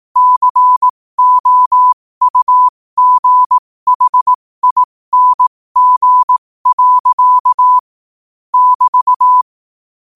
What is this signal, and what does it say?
Morse code, heard as a keyed tone, about 18 wpm, spaced as for 15 wpm, 1000 Hz. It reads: COUGHING. <BT>
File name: qotdfiles / 15wpm